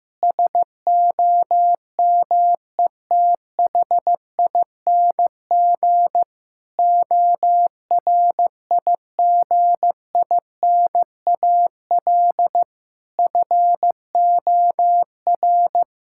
Morse code, heard as a keyed tone, about 15 wpm, 700 Hz. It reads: SOMETHING ORIGINAL FOR
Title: Sentences from Top 1000 Words - Review Part 6 - 15wpm